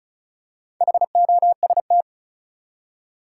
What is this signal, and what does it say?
Morse code, heard as a keyed tone, about 34 wpm, 700 Hz. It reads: HOST
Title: Four Letter Words - 35wpm